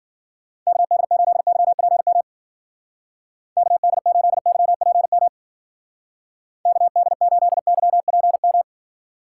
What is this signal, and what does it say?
Morse code, heard as a keyed tone, about 48 wpm, 700 Hz. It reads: KD8YPM KD8YPM KD8YPM
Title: Speed Racing - USA 2x3 Call Signs 40wpm